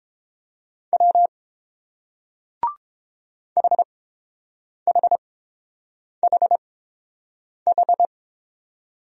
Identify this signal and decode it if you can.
Morse code, heard as a keyed tone, about 29 wpm, spaced as for 7 wpm, 700 Hz. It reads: W HHHH